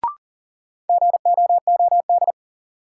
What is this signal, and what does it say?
Morse code, heard as a keyed tone, about 40 wpm, 700 Hz. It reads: GOOD
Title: Sets of 2 Words - New Word Drill - 40wpm